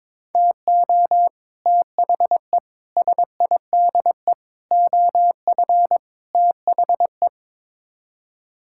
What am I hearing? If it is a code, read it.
Morse code, heard as a keyed tone, about 22 wpm, 700 Hz. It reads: TO THE SIDE OF THE
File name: practice / 22wpm / Build A Phrase - 5-Word Phrases - 22wpm